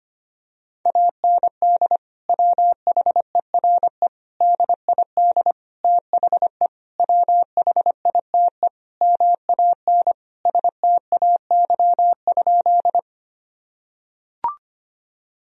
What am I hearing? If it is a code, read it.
Morse code, heard as a keyed tone, about 25 wpm, 700 Hz. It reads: AND WHERE DID THE WHITE MAN STAY?
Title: Sentences from Top 1000 Words - Review Part 1 - 25wpm